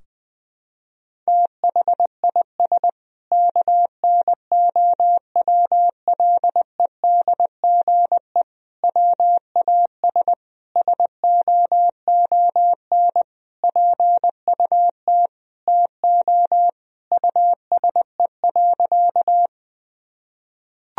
Morse code, heard as a keyed tone, about 20 wpm, 700 Hz. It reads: THIS KNOWLEDGE WAS SOON PUT TO USE.
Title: Sentences from Top 1000 Words - Review Part 3 - 20wpm